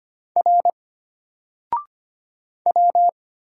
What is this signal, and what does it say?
Morse code, heard as a keyed tone, about 25 wpm, 700 Hz. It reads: R W
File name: Single-Letter-Number-25wpm